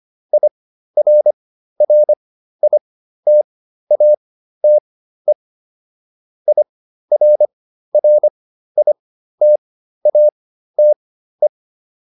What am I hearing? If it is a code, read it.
Morse code, heard as a keyed tone, about 25 wpm, spaced as for 13 wpm, 600 Hz. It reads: IRRITATE IRRITATE